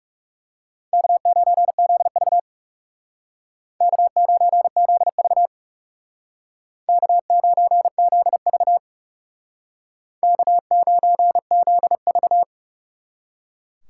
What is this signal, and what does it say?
Morse code, heard as a keyed tone, about 39 wpm, 700 Hz. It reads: K9ZV K9ZV K9ZV K9ZV